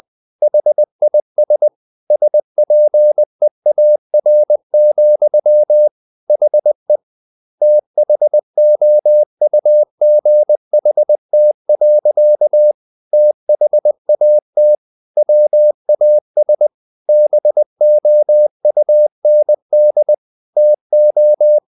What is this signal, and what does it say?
Morse code, heard as a keyed tone, about 20 wpm, 600 Hz. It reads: HIS SPEAR, HE THOUGHT. THAT WAS BOUND TO
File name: the old man and the sea part 2 20wpm